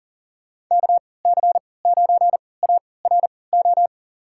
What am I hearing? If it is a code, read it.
Morse code, heard as a keyed tone, about 40 wpm, spaced as for 23 wpm, 700 Hz. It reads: KC9ARO
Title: USA Call Signs - 6 Characters - 3x Character Spacing - 40wpm